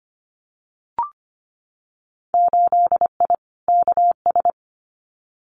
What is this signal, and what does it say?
Morse code, heard as a keyed tone, about 25 wpm, 700 Hz. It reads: 8I KS
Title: Winter Field Day Exchanges Encore 25wpm